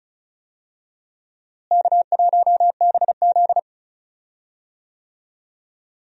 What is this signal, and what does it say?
Morse code, heard as a keyed tone, about 36 wpm, 700 Hz. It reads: K1BZ
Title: USA Call Signs - 4 Characters - Repeated 2x - 35wpm